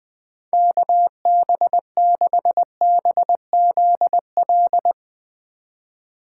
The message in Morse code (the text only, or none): KB6BZL